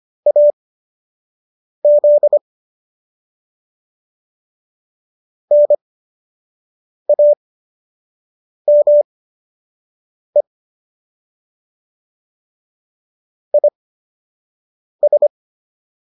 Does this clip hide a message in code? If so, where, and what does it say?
Yes: Morse code from 0.3 to 15.3 s, heard as a keyed tone, about 25 wpm, spaced as for 6 wpm, 590 Hz. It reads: AZ NAME IS